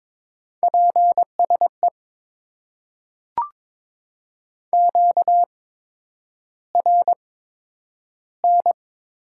Morse code, heard as a keyed tone, about 22 wpm, spaced as for 9 wpm, 700 Hz. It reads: PSE QRN
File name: Top 90 QSO Elements - 8x Character Spacing - 22wpm